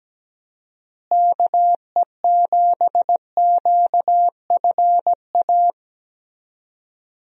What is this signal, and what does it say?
Morse code, heard as a keyed tone, about 17 wpm, 700 Hz. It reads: KE7QFA
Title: USA Call Signs - 6 Characters - Repeated 2x - 17wpm